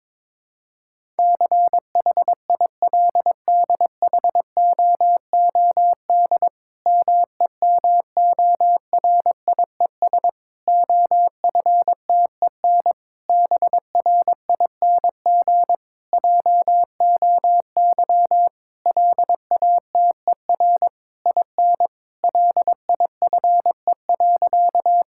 Morse code, heard as a keyed tone, about 22 wpm, 700 Hz. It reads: CHILDHOOD MEMORIES OFTEN BRING JOY LATER IN LIFE.
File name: Difficult Sentences with Hints - 22wpm